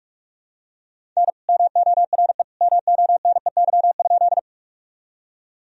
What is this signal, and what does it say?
Morse code, heard as a keyed tone, about 41 wpm, 700 Hz. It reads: N MORE MONEY?
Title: Sentences from Top 400 Words and Abbreviations 45wpm